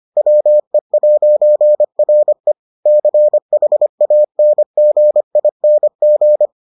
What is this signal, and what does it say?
Morse code, heard as a keyed tone, about 25 wpm, 600 Hz. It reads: WE'RE CHANGING